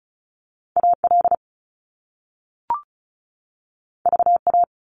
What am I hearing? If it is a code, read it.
Morse code, heard as a keyed tone, about 36 wpm, 700 Hz. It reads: AL VA
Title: US States 35wpm